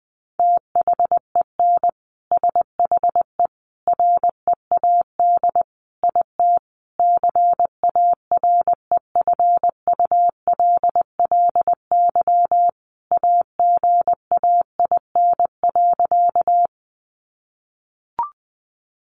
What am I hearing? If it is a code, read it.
Morse code, heard as a keyed tone, about 20 wpm, 700 Hz. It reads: THEN SHE READ IT CAREFULLY AGAIN.
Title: Sentences from Top 1300 Words 20wpm